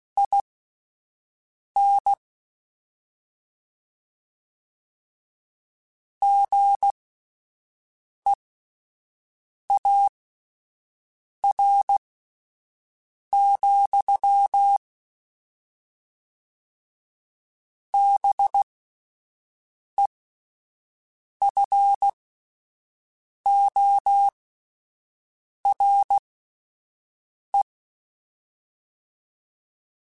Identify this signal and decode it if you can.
Morse code, heard as a keyed tone, about 16 wpm, spaced as for 5 wpm, 800 Hz. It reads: IN GEAR, BEFORE